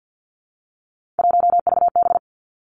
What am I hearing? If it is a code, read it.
Morse code, heard as a keyed tone, about 50 wpm, 700 Hz. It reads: 1VB
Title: Three Challenging Letter-Number Combinations 50wpm